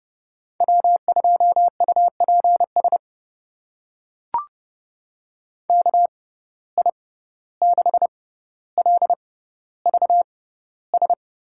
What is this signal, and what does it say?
Morse code, heard as a keyed tone, about 32 wpm, spaced as for 15 wpm, 700 Hz. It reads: W2UPS KI6LVS
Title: USA Call Signs - 6x Character Spacing - 30wpm